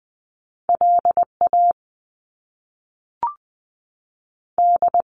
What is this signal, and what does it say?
Morse code, heard as a keyed tone, about 20 wpm, 700 Hz. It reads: L A D
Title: Top Two Letter Combinations in Words 20wpm